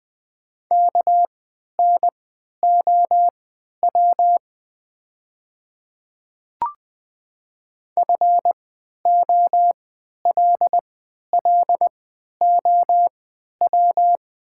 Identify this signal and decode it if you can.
Morse code, heard as a keyed tone, about 20 wpm, spaced as for 11 wpm, 700 Hz. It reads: KNOW FOLLOW
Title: 3x Character Spacing - Top 300 Words - 20wpm